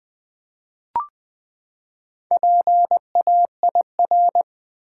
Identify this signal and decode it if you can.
Morse code, heard as a keyed tone, about 20 wpm, 700 Hz. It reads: PAIR
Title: Four Letter Words - 20wpm